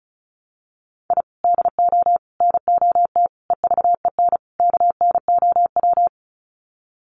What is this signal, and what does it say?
Morse code, heard as a keyed tone, about 34 wpm, 700 Hz. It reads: I DO NOT EVEN KNOW
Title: Sets of 5 Words 35wpm